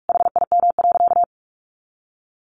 Morse code, heard as a keyed tone, about 44 wpm, 700 Hz. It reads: HIM.
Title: Additional Easy Phrases 45wpm